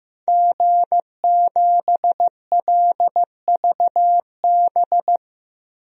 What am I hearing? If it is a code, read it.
Morse code, heard as a keyed tone, about 16 wpm, 700 Hz. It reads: G7LVB
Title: DXCC-100 Call Signs - 5 Characters - 15wpm